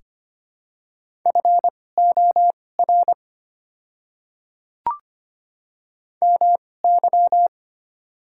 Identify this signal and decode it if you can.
Morse code, heard as a keyed tone, about 25 wpm, spaced as for 18 wpm, 700 Hz. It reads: FOR MY
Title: 2x Character Spacing - Top 200 Words - 25wpm